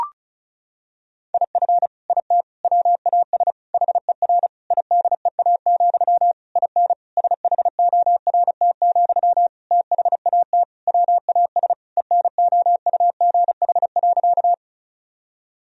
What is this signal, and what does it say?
Morse code, heard as a keyed tone, about 34 wpm, 700 Hz. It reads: IF IT WAS HER IDEA, IN SHORT, THAT WAS ENOUGH.
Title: Sentences from Top 1000 Words - Review Part 2 - 35wpm